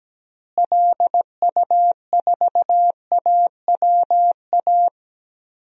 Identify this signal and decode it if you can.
Morse code, heard as a keyed tone, about 17 wpm, 700 Hz. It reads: LU4AWA
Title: DXCC-100 Call Signs - 6 Characters - Repeated 3x - 17wpm